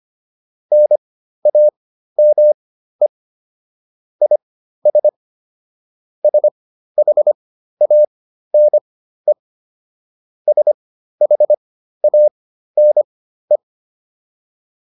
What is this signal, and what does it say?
Morse code, heard as a keyed tone, about 25 wpm, spaced as for 13 wpm, 600 Hz. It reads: NAME IS SHANE SHANE
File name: QSO106 13